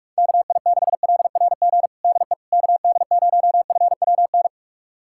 Morse code, heard as a keyed tone, about 41 wpm, 700 Hz. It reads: KI6LRG DE KD0FWN
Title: USA Callsign DE Callsign - 6 Characters - 45wpm